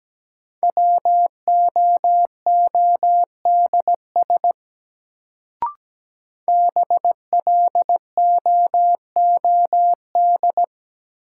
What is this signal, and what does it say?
Morse code, heard as a keyed tone, about 17 wpm, 700 Hz. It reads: WOODS BLOOD
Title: Sentences from Top 1000 Words - Review Part 2 - New Word Drill - 17wpm